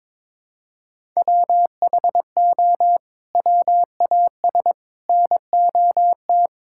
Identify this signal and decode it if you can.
Morse code, heard as a keyed tone, about 22 wpm, 700 Hz. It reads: WHO WAS NOT
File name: practice / 22wpm / Sets of 8 Words - 22wpm